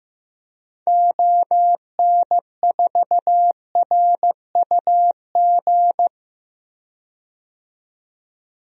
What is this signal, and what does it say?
Morse code, heard as a keyed tone, about 15 wpm, 700 Hz. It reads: ON4RUG
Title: DXCC-100 Call Signs - 6 Characters - 15wpm